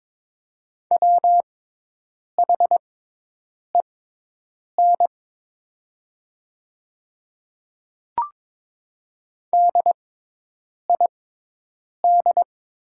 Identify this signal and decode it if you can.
Morse code, heard as a keyed tone, about 22 wpm, spaced as for 8 wpm, 690 Hz. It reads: WHEN DID
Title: 6x Character Spacing - Top 100 Words - 22wpm